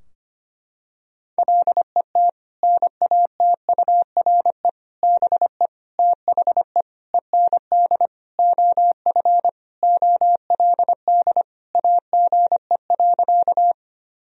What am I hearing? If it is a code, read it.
Morse code, heard as a keyed tone, about 25 wpm, 700 Hz. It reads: LET NATURE BE THE END OF OLD AGE.
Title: Sentences from Top 1000 Words - Review Part 3 - 25wpm